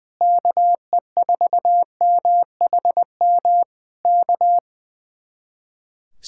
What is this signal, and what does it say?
Morse code, heard as a keyed tone, about 19 wpm, 700 Hz. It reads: KE4MHM K
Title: Speed-Racing Common QSO Phrases Encore - 20wpm